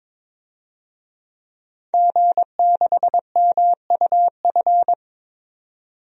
Morse code, heard as a keyed tone, about 21 wpm, 700 Hz. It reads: G6MUF
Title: DXCC-100 Call Signs - 5 Characters - 22wpm